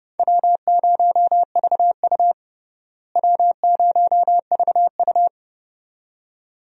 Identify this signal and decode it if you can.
Morse code, heard as a keyed tone, about 32 wpm, 700 Hz. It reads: W0VU W0VU
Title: USA Call Signs - 4 Characters - Repeated 2x - 30wpm